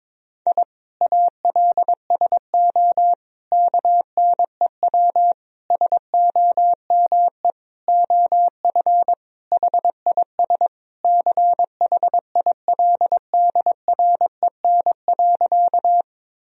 Morse code, heard as a keyed tone, about 22 wpm, 700 Hz. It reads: I ALSO KNEW SOME OF HIS CHILDREN.
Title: Sentences from Top 1000 Words - Review Part 1 - 22wpm